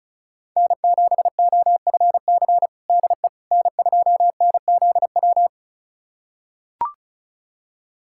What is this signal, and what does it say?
Morse code, heard as a keyed tone, about 36 wpm, 700 Hz. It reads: N7OFC DE N2NZW
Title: USA Callsign DE Callsign - 5 Characters - 35wpm